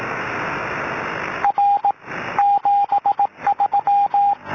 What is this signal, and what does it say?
Morse code, heard as a keyed tone, about 18 wpm, 810 Hz, with radio noise. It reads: R 73
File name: 4U1ITU